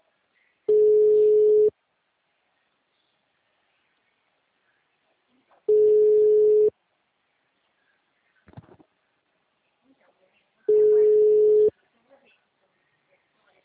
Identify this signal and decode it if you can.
Ringback tone (Europe, 425 Hz), with speech.